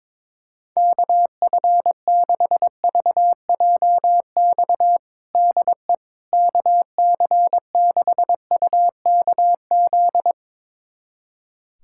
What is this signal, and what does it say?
Morse code, heard as a keyed tone, about 22 wpm, 700 Hz. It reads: KF6VJX DE KC6UKZ
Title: USA Callsign DE Callsign - 6 Characters - 22wpm